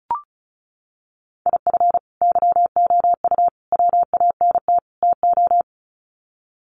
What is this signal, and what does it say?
Morse code, heard as a keyed tone, about 34 wpm, 700 Hz. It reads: IF YOU WANT TO
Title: Sets of 4 Words 35wpm